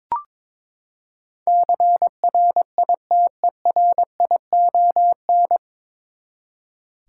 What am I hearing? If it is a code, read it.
Morse code, heard as a keyed tone, about 21 wpm, 700 Hz. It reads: CRITERION
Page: Morse Code Ninja practice 22wpm